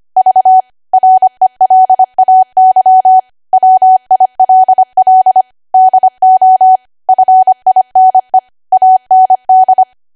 Morse code, heard as a keyed tone, about 25 wpm, 750 Hz. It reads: V RELAY WILL DO FINE AND